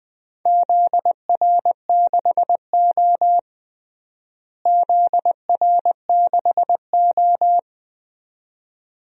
Morse code, heard as a keyed tone, about 20 wpm, 700 Hz. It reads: ZR6O ZR6O
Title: DXCC-100 Call Signs - 4 Characters - Repeated 2x - 20wpm